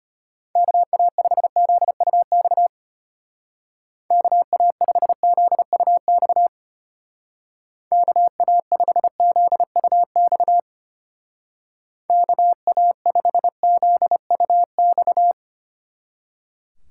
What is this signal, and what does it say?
Morse code, heard as a keyed tone, about 31 wpm, 700 Hz. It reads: KA5ZUX KA5ZUX KA5ZUX KA5ZUX
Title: Speed Racing - USA 2x3 Call Signs 25wpm